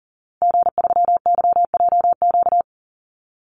Morse code, heard as a keyed tone, about 40 wpm, 700 Hz. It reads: G3YJQ
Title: Call Signs Repeated 2x - 40wpm